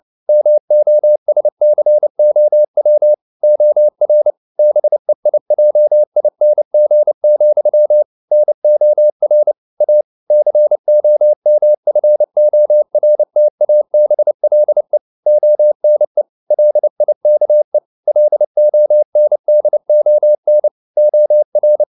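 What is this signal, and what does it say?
Morse code, heard as a keyed tone, about 29 wpm, 600 Hz. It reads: MOSCOW OR BEIJING, NOR A COMFORTABLE ONE LIKE LONDON OR
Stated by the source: Morse code at 29 wpm, sending Ken Follett's novel Never in English, from CW med Gnister.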